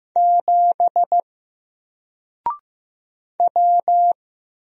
Morse code, heard as a keyed tone, about 15 wpm, 700 Hz. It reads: 7 W
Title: Single Letter-Number Alt 15wpm